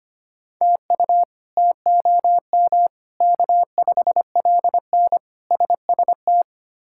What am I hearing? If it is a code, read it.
Morse code, heard as a keyed tone, about 25 wpm, 700 Hz. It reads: TU TOM K5LN SST
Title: K1USN SST QSO Practice - 25wpm